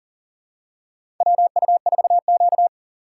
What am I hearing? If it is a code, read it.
Morse code, heard as a keyed tone, about 40 wpm, 700 Hz. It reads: WU4Q
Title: USA Call Signs - 4 Characters - 3x Character Spacing - 40wpm